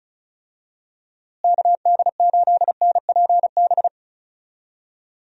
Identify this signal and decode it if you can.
Morse code, heard as a keyed tone, about 36 wpm, 700 Hz. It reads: KD8NPB
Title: USA Call Signs - 6 Characters - Repeated 3x - 35wpm